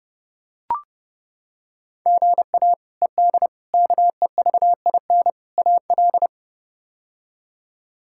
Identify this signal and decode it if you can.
Morse code, heard as a keyed tone, about 30 wpm, 700 Hz. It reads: GA ED KEVIN AL
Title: K1USN SST QSO Practice - 30wpm